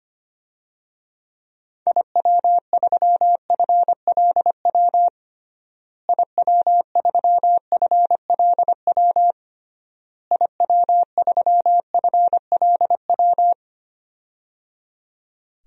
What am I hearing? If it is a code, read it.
Morse code, heard as a keyed tone, about 25 wpm, 700 Hz. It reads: IW3FLW IW3FLW IW3FLW
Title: DXCC-100 Call Signs - 6 Characters - Repeated 3x - 25wpm